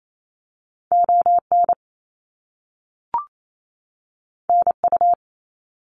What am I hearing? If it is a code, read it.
Morse code, heard as a keyed tone, about 28 wpm, 700 Hz. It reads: ON NU
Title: Canadian Provinces and Territories 28wpm